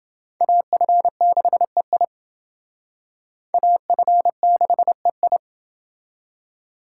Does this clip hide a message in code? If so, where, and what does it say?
Yes: Morse code from 0.4 to 5.4 s, heard as a keyed tone, about 28 wpm, 700 Hz. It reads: AF6EI AF6EI